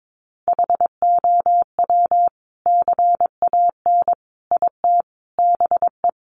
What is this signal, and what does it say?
Morse code, heard as a keyed tone, about 22 wpm, 700 Hz. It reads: HOW CAN IT BE
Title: Sentences from Top 1400 Words 22wpm